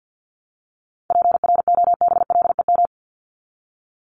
Morse code, heard as a keyed tone, about 50 wpm, 700 Hz. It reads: PROBLEM
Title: Top 500 Words Repeated 50wpm